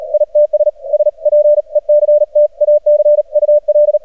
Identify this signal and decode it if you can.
Morse code, heard as a keyed tone, about 40 wpm, 600 Hz. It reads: ITS SPECTACUL
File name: archived / CRSnet